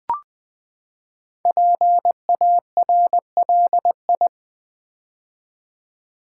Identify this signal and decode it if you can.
Morse code, heard as a keyed tone, about 20 wpm, 700 Hz. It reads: PARLI
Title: Build A Word - 5 Letters At A Time - 10-Letter Words - 20wpm